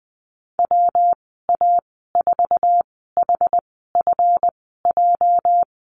Morse code, heard as a keyed tone, about 20 wpm, spaced as for 14 wpm, 700 Hz. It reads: WA4HFJ
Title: 245 Lesson 39g - Letter z - Call Signs 20wpm